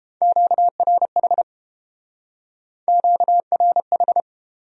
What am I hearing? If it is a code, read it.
Morse code, heard as a keyed tone, about 31 wpm, 700 Hz. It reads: QRH QRH